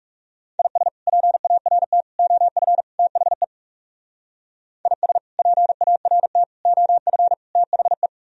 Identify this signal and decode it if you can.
Morse code, heard as a keyed tone, about 40 wpm, 700 Hz. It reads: IS PART OF THE IS PART OF THE